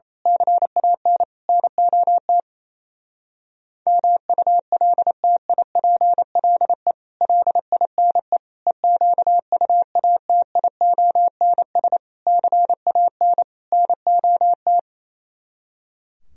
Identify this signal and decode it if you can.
Morse code, heard as a keyed tone, about 26 wpm, 700 Hz. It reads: CAN NOT MULTIPLE LINE EQUATIONS CAN NOT
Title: Speed-Racing - Sets of 5 Words 28wpm